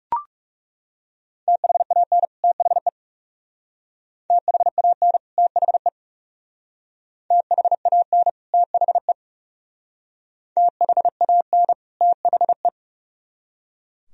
Morse code, heard as a keyed tone, about 38 wpm, 700 Hz. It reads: THAN THE THAN THE THAN THE THAN THE
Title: Speed-Racing - Sets of 2 Words 30wpm